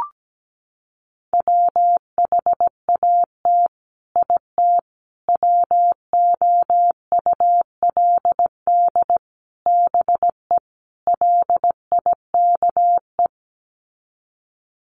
Morse code, heard as a keyed tone, about 17 wpm, 700 Hz. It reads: WHAT IT WOULD BE LIKE
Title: Sets of 5 Words Encore 17wpm